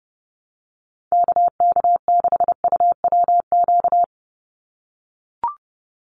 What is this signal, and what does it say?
Morse code, heard as a keyed tone, about 30 wpm, 700 Hz. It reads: KK6UWQ
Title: USA Call Signs 30wpm